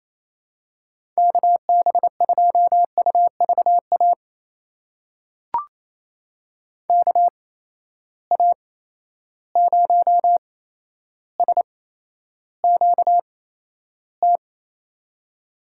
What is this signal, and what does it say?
Morse code, heard as a keyed tone, about 28 wpm, spaced as for 12 wpm, 700 Hz. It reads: KB2UVA KA0SQT
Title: USA Call Signs - 6 Characters - 8x Character Spacing - 28wpm